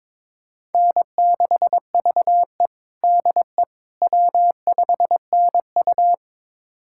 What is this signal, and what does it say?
Morse code, heard as a keyed tone, about 22 wpm, 700 Hz. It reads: N6VE DE W5NU
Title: USA Callsign DE Callsign - 4 Characters - 22wpm